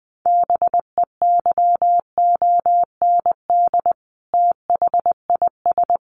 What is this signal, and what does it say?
Morse code, heard as a keyed tone, about 20 wpm, 700 Hz. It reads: BEYOND THIS
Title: A Sound Of Thunder 20wpm